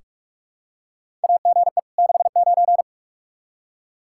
Morse code, heard as a keyed tone, about 46 wpm, 700 Hz. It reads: AGE 69
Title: Practice QSO - Exchange 3 - More Info - 45wpm